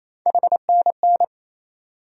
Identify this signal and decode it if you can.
Morse code, heard as a keyed tone, about 27 wpm, 700 Hz. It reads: HNN
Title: USA Call Sign Postfixes - 3 Letters 28wpm